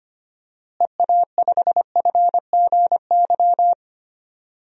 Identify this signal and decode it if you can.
Morse code, heard as a keyed tone, about 25 wpm, 700 Hz. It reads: EA5FGY
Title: DXCC-100 Call Signs - 6 Characters - Repeated 2x - 25wpm